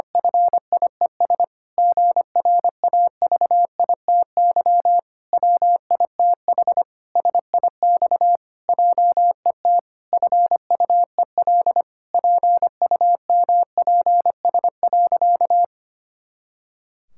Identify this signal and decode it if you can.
Morse code, heard as a keyed tone, about 25 wpm, 700 Hz. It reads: FIES GRAVITY WITH SIX JET FUEL PUMPS.
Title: Pangram Challenge v2 - 25wpm